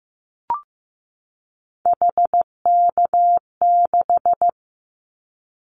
Morse code, heard as a keyed tone, about 15 wpm, 700 Hz. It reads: HK6
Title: Callsign Prefixes for Top 100 DXCC - 15wpm